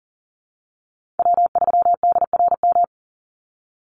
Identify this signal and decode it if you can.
Morse code, heard as a keyed tone, about 39 wpm, 700 Hz. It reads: W3DRM
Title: USA Call Signs 40wpm